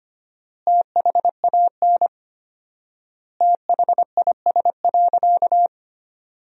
Morse code, heard as a keyed tone, about 25 wpm, 700 Hz. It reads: THAN THIS.
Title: 4x Word-Spacing - Sentences from Top 100 Words - 25wpm